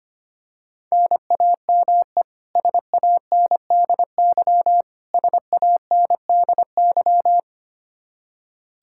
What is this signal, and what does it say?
Morse code, heard as a keyed tone, about 25 wpm, 700 Hz. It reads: NAME SANDY SANDY
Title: K1USN SST QSO Practice 25wpm